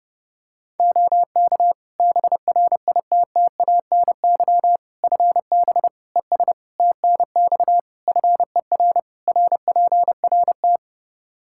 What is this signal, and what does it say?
Morse code, heard as a keyed tone, about 30 wpm, 700 Hz. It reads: OK BRITTANY FB ES TNX FER RPRT
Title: K1USN SST QSO Practice 30wpm